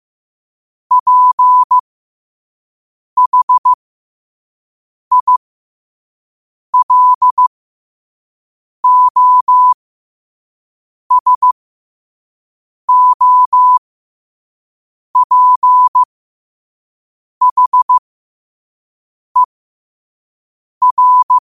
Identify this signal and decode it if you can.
Morse code, heard as a keyed tone, about 15 wpm, spaced as for 5 wpm, 1000 Hz. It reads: PHILOSOPHER
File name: qotdfiles / 5wpm